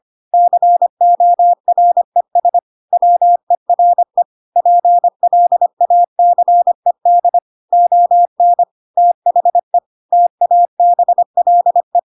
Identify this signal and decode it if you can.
Morse code, heard as a keyed tone, about 25 wpm, 700 Hz. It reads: CORES WERE PLACED ON THE TABLE
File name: Pinocchio / Pinocchio ch7 25 25 ik1hge